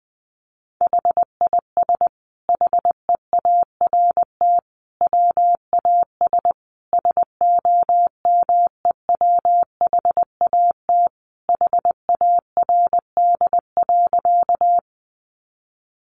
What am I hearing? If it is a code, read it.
Morse code, heard as a keyed tone, about 20 wpm, 700 Hz. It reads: HIS HEART WAS SOMEWHAT HARD.
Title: Sentences from Top 1300 Words 20wpm